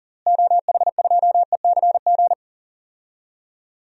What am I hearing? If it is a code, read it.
Morse code, heard as a keyed tone, about 42 wpm, 700 Hz. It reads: OH2ECG
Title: DXCC-100 Call Signs - 6 Characters - Repeated 3x - 40wpm